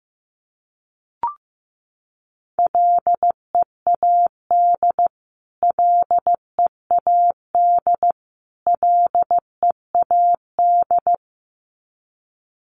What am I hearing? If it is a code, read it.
Morse code, heard as a keyed tone, about 15 wpm, 700 Hz. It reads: LEAD LEAD LEAD
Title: Top 500 Words Repeated 15wpm